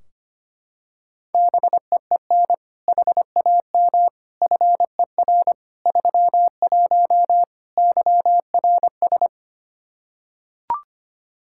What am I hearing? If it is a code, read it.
Morse code, heard as a keyed tone, about 25 wpm, 700 Hz. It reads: BEEN HAM FER 31 YRS
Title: Practice QSO - Exchange 3 - More Info - 25wpm